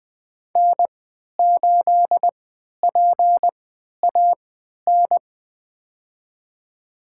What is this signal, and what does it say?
Morse code, heard as a keyed tone, about 21 wpm, spaced as for 11 wpm, 700 Hz. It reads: N8PAN
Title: USA Call Signs - 5 Characters - 3x Character Spacing - 20wpm